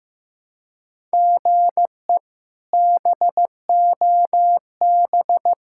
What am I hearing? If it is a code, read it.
Morse code, heard as a keyed tone, about 15 wpm, 700 Hz. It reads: GE BOB
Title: K1USN SST QSO Practice - 15wpm